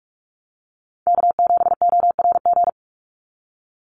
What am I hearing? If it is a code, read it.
Morse code, heard as a keyed tone, about 44 wpm, 690 Hz. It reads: K7ORG
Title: NA SOTA Chaser Callsigns 45wpm